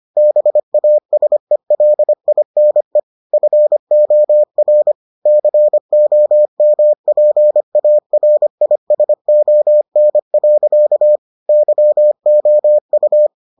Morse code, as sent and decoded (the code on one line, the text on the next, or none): -... .- ... .-.. .. -. / ..-. --- .-. / -.-. --- -- .--. .- .-. .. ... --- -. .-.-.- / -.-- --- ..-
BASELINE FOR COMPARISON. YOU